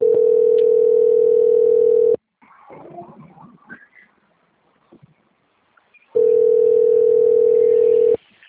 Ringback tone (North America), 440 + 480 Hz.
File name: out- - - - - .1871